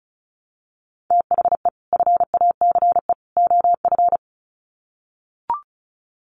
Morse code, heard as a keyed tone, about 34 wpm, 700 Hz. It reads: THE FACE OF
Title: Sets of 3 Words 35wpm